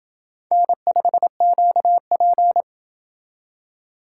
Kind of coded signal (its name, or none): Morse code